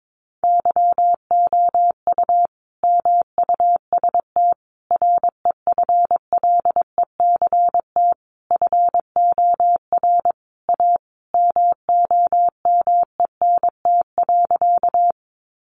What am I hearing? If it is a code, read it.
Morse code, heard as a keyed tone, about 22 wpm, 700 Hz. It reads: YOU MUST REFLECT FOR A MOMENT.
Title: Sentences from Top 1500 Words 22wpm